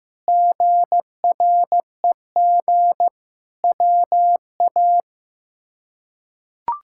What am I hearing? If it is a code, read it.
Morse code, heard as a keyed tone, about 15 wpm, 700 Hz. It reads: GREG WA
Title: K1USN SST QSO Practice - 15wpm